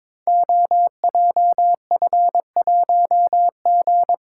Morse code, heard as a keyed tone, about 23 wpm, 700 Hz. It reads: OJF1G
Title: Code Groups - 5 Letter:Numbers - 22wpm